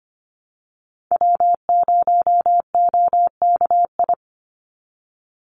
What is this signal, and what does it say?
Morse code, heard as a keyed tone, about 25 wpm, 700 Hz. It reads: W0OKI
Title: USA Call Signs 25wpm